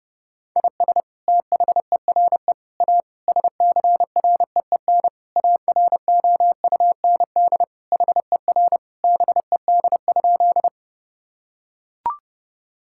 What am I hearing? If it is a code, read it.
Morse code, heard as a keyed tone, about 32 wpm, 700 Hz. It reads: IS THERE A SCREEN AROUND HER BED?
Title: Sentences from Top 1000 Words - Review Part 6 - 30wpm